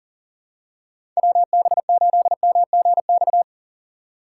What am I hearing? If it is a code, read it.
Morse code, heard as a keyed tone, about 39 wpm, 700 Hz. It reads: WB8MGX
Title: USA Call Signs - 6 Characters - 3x Character Spacing - 40wpm